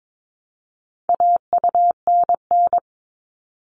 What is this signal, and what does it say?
Morse code, heard as a keyed tone, about 22 wpm, 700 Hz. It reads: AUNN
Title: All Cut Numbers - 4-Digit 22wpm